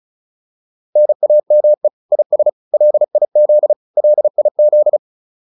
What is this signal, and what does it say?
Morse code, heard as a keyed tone, about 34 wpm, 600 Hz. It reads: NAME IS LIZ LIZ